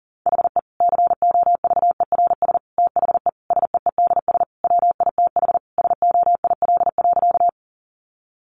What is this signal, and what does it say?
Morse code, heard as a keyed tone, about 40 wpm, 700 Hz. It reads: HE COVERS THE SEEDS WITH SOIL.
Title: Easy Phrases 40wpm